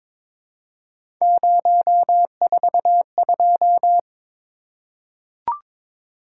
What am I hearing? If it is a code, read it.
Morse code, heard as a keyed tone, about 23 wpm, 700 Hz. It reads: 042
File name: practice / 22wpm / Three Letter-Number Combinations - 22wpm